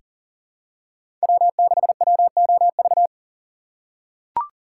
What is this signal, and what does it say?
Morse code, heard as a keyed tone, about 40 wpm, 700 Hz. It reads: W6WOV
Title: USA Call Signs - 5 Characters - 3x Character Spacing - 40wpm